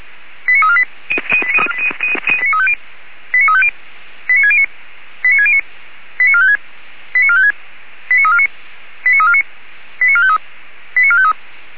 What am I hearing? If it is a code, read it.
Five-tone selective call (ZVEI), heard as quick runs of five tones, tones of about 70 ms. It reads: 89348 89350 89609 89609 89456 89456 89349 89349 89453 89453